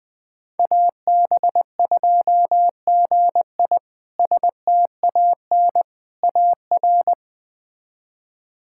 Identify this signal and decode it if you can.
Morse code, heard as a keyed tone, about 20 wpm, 700 Hz. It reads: AB2GI STAN AR
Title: K1USN SST QSO Practice - 20wpm